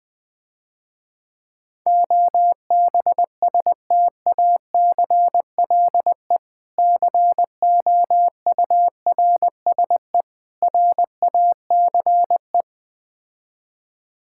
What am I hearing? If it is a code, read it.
Morse code, heard as a keyed tone, about 20 wpm, 700 Hz. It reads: OBSTACLE COURSE RACE
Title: Difficult Words - Speed-Racing with Hints - 20wpm